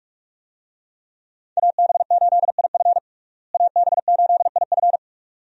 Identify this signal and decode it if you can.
Morse code, heard as a keyed tone, about 41 wpm, 700 Hz. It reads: AB8IF AB8IF
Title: USA Call Signs - 5 Characters - Repeated 2x - 45wpm